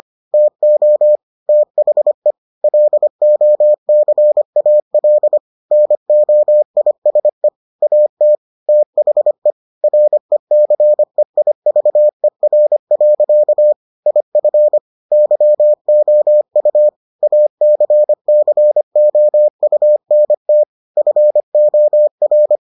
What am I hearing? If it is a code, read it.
Morse code, heard as a keyed tone, about 25 wpm, 600 Hz. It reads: TO THE LOCAL NOISE AT THE RECEIVER. IF YOU ACCOUNT FOR